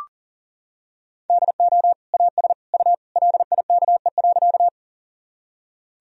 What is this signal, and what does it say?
Morse code, heard as a keyed tone, about 40 wpm, 700 Hz. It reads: DO AS U LIKE.
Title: Sentences from Top 100 Words and Abbreviations 40wpm